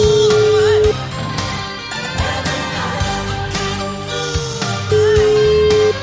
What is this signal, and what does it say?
Ringback tone (Europe, 425 Hz), with noise.